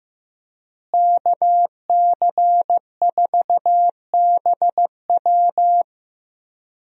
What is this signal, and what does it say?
Morse code, heard as a keyed tone, about 15 wpm, 700 Hz. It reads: KC4BW
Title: USA Call Signs - 5 Characters - Repeated 3x - 15wpm